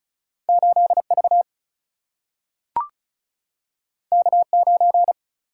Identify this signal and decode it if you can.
Morse code, heard as a keyed tone, about 36 wpm, 700 Hz. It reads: 8 V K 9
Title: Single-Letter Single-Number 35wpm